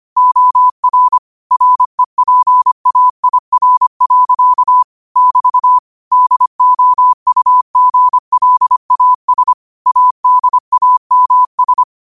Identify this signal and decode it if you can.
Morse code, heard as a keyed tone, about 25 wpm, 1000 Hz. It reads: OR REPAIR. <BT> DOUGLAS ADAMS
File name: qotdfiles / 25wpm